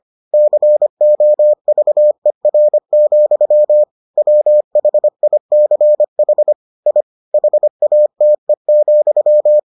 Morse code, heard as a keyed tone, about 25 wpm, 600 Hz. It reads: COVER, WHICH I HATE,